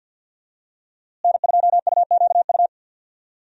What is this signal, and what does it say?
Morse code, heard as a keyed tone, about 51 wpm, 700 Hz. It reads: N2UQU
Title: USA Call Signs - 5 Characters - Repeated 2x - 50wpm